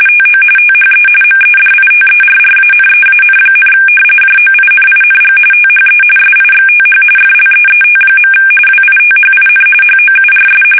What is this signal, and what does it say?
RTTY, about 45.45 baud. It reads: TO THE MEDITERRANEAN, JOINING THE SIXTH FLEET. WITHIN HOURS
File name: fsk850 / battleshipnj